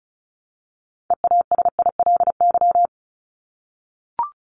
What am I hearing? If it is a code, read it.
Morse code, heard as a keyed tone, about 34 wpm, 700 Hz. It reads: EASILY
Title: LY Suffix - Top 100 Words with Base Words 35wpm